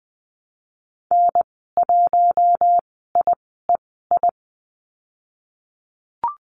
Morse code, heard as a keyed tone, about 19 wpm, spaced as for 14 wpm, 700 Hz. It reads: N1IEI
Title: 076 Lesson 13g - Letter L - Call Signs 20wpm